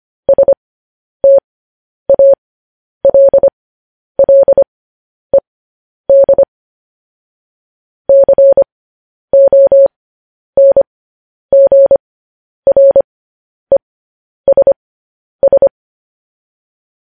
Morse code, heard as a keyed tone, about 25 wpm, spaced as for 10 wpm, 550 Hz. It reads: STALLED CONGRESS